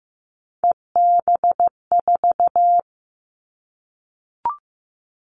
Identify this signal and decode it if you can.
Morse code, heard as a keyed tone, about 15 wpm, 700 Hz. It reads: EB4
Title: Callsign Prefixes for Top 25 DXCC - Repeated 3x - 15wpm